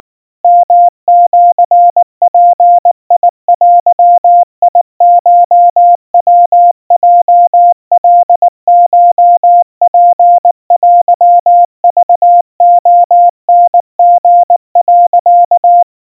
Morse code (Japanese Wabun code), heard as a keyed tone, about 19 wpm, 700 Hz. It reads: ヨシヅデコヤヲカコツテクレタリ、
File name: kawa / Morse sample / kamyu 70 cpm 700 Hz Crispy